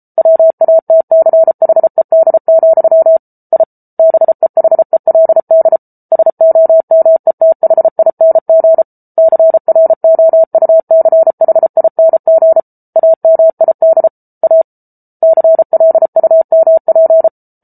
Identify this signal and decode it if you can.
Morse code, heard as a keyed tone, about 34 wpm, 650 Hz. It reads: WATCHED, I BEHELD SOMETHING CROUCHING AMID A CLUMP